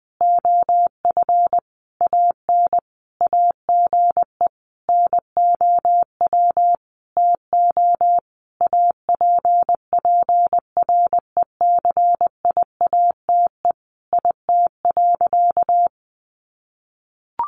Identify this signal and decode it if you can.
Morse code, heard as a keyed tone, about 20 wpm, 700 Hz. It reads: OF AN AGE NOW TO APPRECIATE IT.
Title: Sentences from Top 1300 Words 20wpm